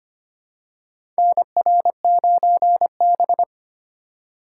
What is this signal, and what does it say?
Morse code, heard as a keyed tone, about 25 wpm, 700 Hz. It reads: NR9B
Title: USA Call Signs - 4 Characters - 4x Character Spacing - 25wpm